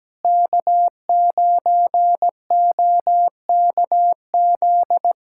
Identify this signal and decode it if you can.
Morse code, heard as a keyed tone, about 17 wpm, 700 Hz. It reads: K9OKZ
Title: USA Callsign DE Callsign - 5 Characters - 17wpm